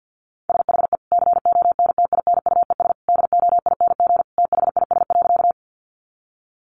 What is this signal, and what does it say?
Morse code, heard as a keyed tone, about 50 wpm, 700 Hz. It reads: SHE CONTINUES DOING THIS.
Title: Easy Phrases 50wpm